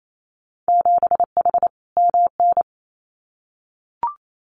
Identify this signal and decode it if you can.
Morse code, heard as a keyed tone, about 28 wpm, 700 Hz. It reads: 7H MN
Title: Winter Field Day Exchanges Encore 28wpm